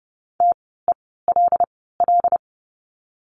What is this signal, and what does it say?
Morse code, heard as a keyed tone, about 30 wpm, spaced as for 17 wpm, 700 Hz. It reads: TELL
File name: course / 30:10wpm / 108 Lesson 18f - Letter m - Words 30wpm